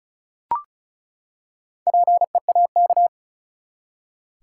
Morse code, heard as a keyed tone, about 36 wpm, 700 Hz. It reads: PEAK
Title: Four Letter Words - 35wpm